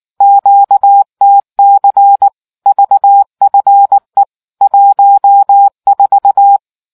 Morse code, heard as a keyed tone, about 19 wpm, 800 Hz. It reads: QTC VFE 14